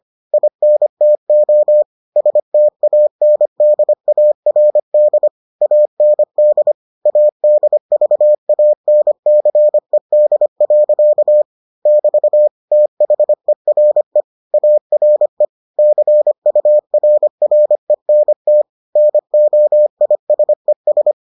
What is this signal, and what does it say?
Morse code, heard as a keyed tone, about 25 wpm, 600 Hz. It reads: INTO STANDARD AND ADVANCED. <BT> THERE ARE CURRENT NOISES